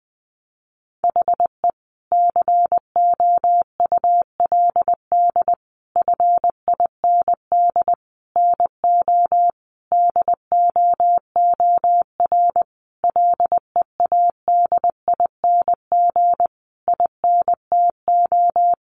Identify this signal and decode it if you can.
Morse code, heard as a keyed tone, about 20 wpm, 700 Hz. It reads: HE COULD FIND NO DOOR LEADING INTO
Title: Sentences from Top 1100 Words 20wpm